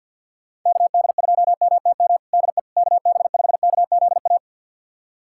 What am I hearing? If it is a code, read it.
Morse code, heard as a keyed tone, about 50 wpm, 700 Hz. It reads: KD2MTM DE KB5KZA
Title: USA Callsign DE Callsign - 6 Characters - 50wpm